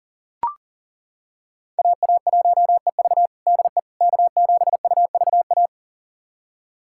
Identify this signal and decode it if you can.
Morse code, heard as a keyed tone, about 40 wpm, 700 Hz. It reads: AA1EV DE K7UVA
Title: USA Callsign DE Callsign - 5 Characters - 40wpm